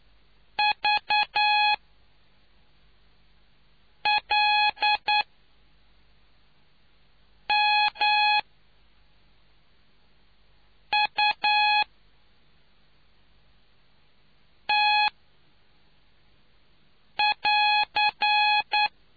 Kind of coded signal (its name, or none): Morse code